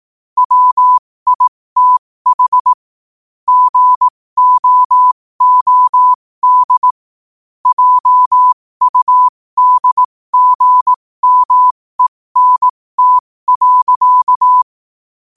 Morse code, heard as a keyed tone, about 18 wpm, spaced as for 15 wpm, 1000 Hz. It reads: WITH GOOD JUDGMENT.
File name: qotdfiles / 15wpm